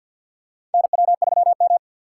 Morse code, heard as a keyed tone, about 52 wpm, 700 Hz. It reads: NW3M
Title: USA Call Signs - 4 Characters - 6x Character Spacing - 50wpm